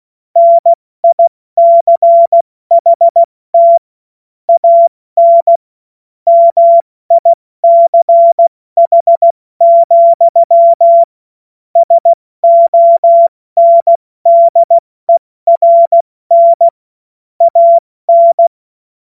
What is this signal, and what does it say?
Morse code, heard as a keyed tone, about 16 wpm, 680 Hz. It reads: NICHT AN MICH, SONDERN AN
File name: Morsekurs / Witze / Witz 20. Juni 2023 70 F80